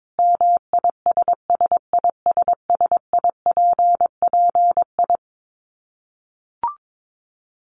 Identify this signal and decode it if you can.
Morse code, heard as a keyed tone, about 22 wpm, 700 Hz. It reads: MISSISSIPPI
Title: Dit Challenge 22wpm